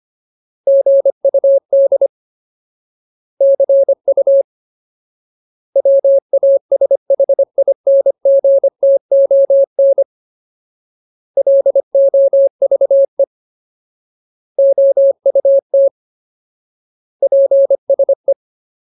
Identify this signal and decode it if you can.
Morse code, heard as a keyed tone, about 25 wpm, 550 Hz. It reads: GUD CU WASHINGTON LOVE OUT PSE
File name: Ultimate Word List / 25WPM Ultimate